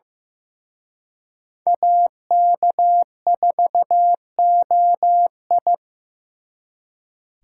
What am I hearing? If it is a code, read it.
Morse code, heard as a keyed tone, about 18 wpm, 700 Hz. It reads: AK4OI
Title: Speed Racing - USA Call Signs 15wpm